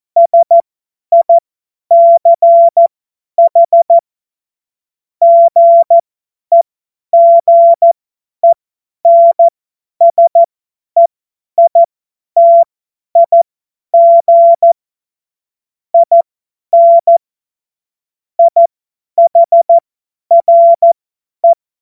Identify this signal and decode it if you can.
Morse code, heard as a keyed tone, about 14 wpm, spaced as for 10 wpm, 680 Hz. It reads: SICH GEGENSEITIG IN IHRE
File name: Morsekurs / Witze / Witz 05 Dezember 2023 50 F70